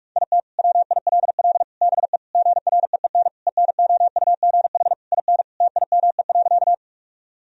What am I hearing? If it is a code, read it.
Morse code, heard as a keyed tone, about 46 wpm, 700 Hz. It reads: IT WILL BE GREEN ENOUGH IN TIME.
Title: Sentences from Top 1000 Words - Review Part 2 - 45wpm